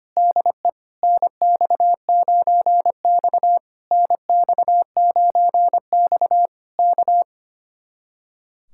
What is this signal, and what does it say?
Morse code, heard as a keyed tone, about 25 wpm, 700 Hz. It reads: DE NX9X NX9X K
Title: Speed-Racing Common QSO Phrases - 25wpm